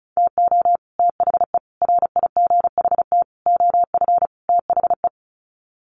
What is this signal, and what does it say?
Morse code, heard as a keyed tone, about 34 wpm, 700 Hz. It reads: TO THE RIGHT OF THE
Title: Sets of 5 Words Encore 35wpm